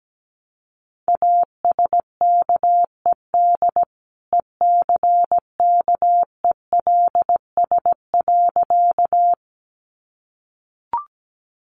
Morse code, heard as a keyed tone, about 17 wpm, 700 Hz. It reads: ASKED ECKELS.
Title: A Sound Of Thunder 17wpm